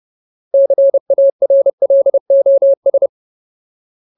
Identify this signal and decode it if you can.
Morse code, heard as a keyed tone, about 30 wpm, 550 Hz. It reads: CARLOS
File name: Ultimate Word List / 30WPM Ultimate